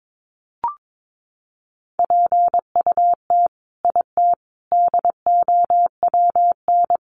Morse code, heard as a keyed tone, about 22 wpm, 700 Hz. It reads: PUT IT DOWN
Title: Sentences from Top 900 Words 22wpm